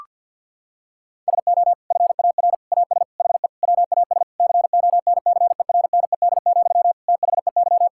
Morse code, heard as a keyed tone, about 50 wpm, 700 Hz. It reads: SO FAR AS HE WAS CONCERNED, THEY
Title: Sentences from Top 1000 Words - Review Part 6 - 50wpm